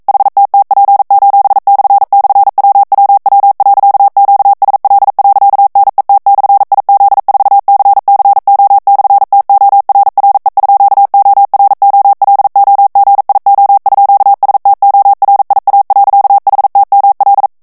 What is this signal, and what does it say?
Morse code, heard as a keyed tone, about 42 wpm, 800 Hz. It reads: HTTP://WWW.QSL.NET/IZ4CCO/TORRE_OROLOGIO_STORIA.HTML
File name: ITALIEN texte clair 1 vitesse 40 mots minute